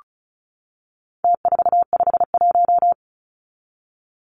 Morse code, heard as a keyed tone, about 34 wpm, 700 Hz. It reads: T451
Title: Numbers - Contest Style - 4-Digit 35wpm